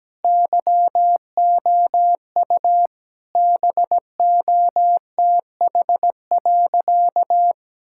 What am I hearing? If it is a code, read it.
Morse code, heard as a keyed tone, about 17 wpm, 700 Hz. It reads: YOU BOTH.
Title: Sentences from Top 1000 Words - Review Part 6 - 17wpm